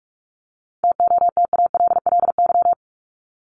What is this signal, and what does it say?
Morse code, heard as a keyed tone, about 44 wpm, 700 Hz. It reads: TOTALLY
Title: LY Suffix - Top 100 Words with Base Words 45wpm